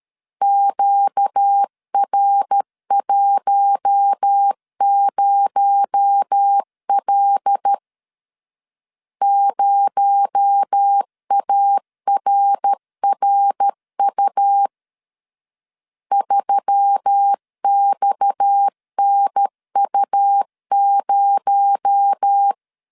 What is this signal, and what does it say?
Morse code, heard as a keyed tone, about 13 wpm, 800 Hz. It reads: QR10L 0ARRU 3XNU0
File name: CW / DET / 1 6 1500C S B 12wpm 3431